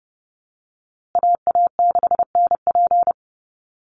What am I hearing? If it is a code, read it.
Morse code, heard as a keyed tone, about 30 wpm, 700 Hz. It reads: AA6NP
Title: USA Call Signs Repeated 3x 30wpm